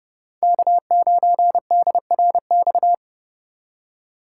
Morse code, heard as a keyed tone, about 30 wpm, 700 Hz. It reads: K9DRX
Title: USA Call Signs - 5 Characters - 2x Character Spacing - 30wpm